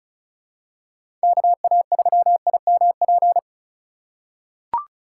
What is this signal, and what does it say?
Morse code, heard as a keyed tone, about 34 wpm, 700 Hz. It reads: KA3IMP
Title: USA Call Signs - 6 Characters - 2x Character Spacing - 35wpm